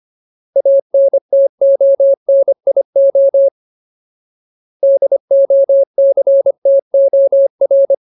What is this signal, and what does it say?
Morse code, heard as a keyed tone, about 25 wpm, 550 Hz. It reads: ANTONIO DOCTOR
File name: Ultimate Word List / 25WPM Ultimate